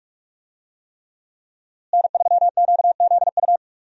Morse code, heard as a keyed tone, about 47 wpm, 700 Hz. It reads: N3QZU
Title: USA Call Signs - 5 Characters - Repeated 3x - 45wpm